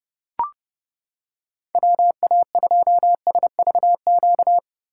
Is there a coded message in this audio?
Yes: Morse code from 1.7 to 4.6 s, heard as a keyed tone, about 28 wpm, 700 Hz. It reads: WA2SVQ